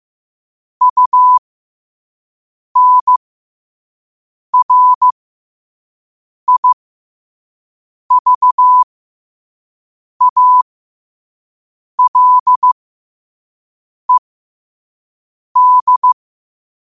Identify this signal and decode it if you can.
Morse code, heard as a keyed tone, about 15 wpm, spaced as for 5 wpm, 1000 Hz. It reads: UNRIVALED
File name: qotdfiles / 5wpm